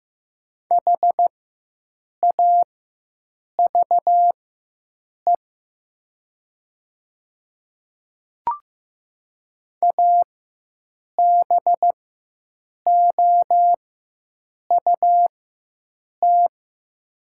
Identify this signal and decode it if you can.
Morse code, heard as a keyed tone, about 15 wpm, spaced as for 7 wpm, 700 Hz. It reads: HAVE ABOUT
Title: 4x Character Spacing - Top 100 Words - 15wpm